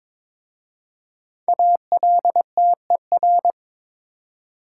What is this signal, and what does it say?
Morse code, heard as a keyed tone, about 23 wpm, 700 Hz. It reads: ALTER
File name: 5 Letter Words - First 3 Letters Given - 22wpm